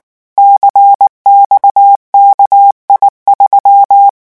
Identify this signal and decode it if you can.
Morse code, heard as a keyed tone, about 19 wpm, 800 Hz. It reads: CXKI3
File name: morse mp3 / revision10-vitesse 18 mots